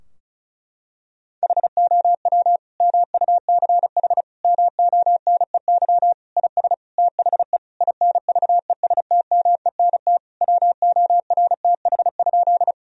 Morse code, heard as a keyed tone, about 36 wpm, 700 Hz. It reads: HOW MUCH MONEY IS THE INVESTMENT WORTH?
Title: Sentences from Top 1000 Words - Review Part 4 - 35wpm